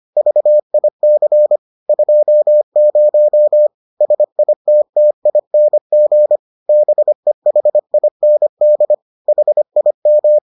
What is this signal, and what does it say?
Morse code, heard as a keyed tone, about 25 wpm, 600 Hz. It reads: VIC 20 SITTING BEHIND HIM